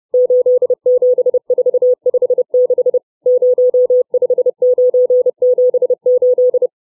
Morse code, heard as a keyed tone, about 30 wpm, 500 Hz. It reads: 87456 05978
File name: CW / letter number groups 4